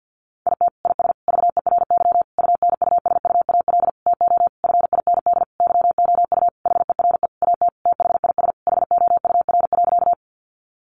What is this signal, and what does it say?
Morse code, heard as a keyed tone, about 50 wpm, 700 Hz. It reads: IT IS VERY UNUSUAL TO FIND YOU HERE AT THIS HOUR.
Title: Sentences from Top 1400 Words 50wpm